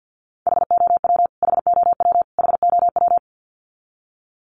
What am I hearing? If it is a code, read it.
Morse code, heard as a keyed tone, about 50 wpm, 700 Hz. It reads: HOW HOW HOW
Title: Top 200 Words Repeated 50wpm